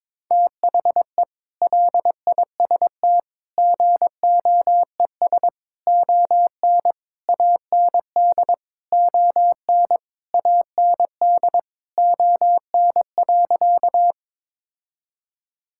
Morse code, heard as a keyed tone, about 22 wpm, 700 Hz. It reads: THE LIST GOES ON AND ON AND ON.
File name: practice / 22wpm / Sentences with 8-Word Phrases - 22wpm